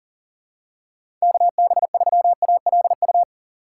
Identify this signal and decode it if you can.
Morse code, heard as a keyed tone, about 40 wpm, 700 Hz. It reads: KB3ALU
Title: USA Call Signs - 6 Characters - 4x Character Spacing - 40wpm